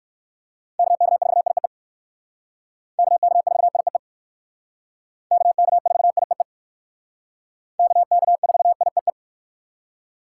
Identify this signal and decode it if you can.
Morse code, heard as a keyed tone, about 57 wpm, 700 Hz. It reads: KK4IEE KK4IEE KK4IEE KK4IEE